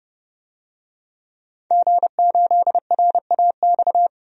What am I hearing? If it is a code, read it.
Morse code, heard as a keyed tone, about 30 wpm, 700 Hz. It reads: G8RAX
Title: DXCC-100 Call Signs - 5 Characters - 30wpm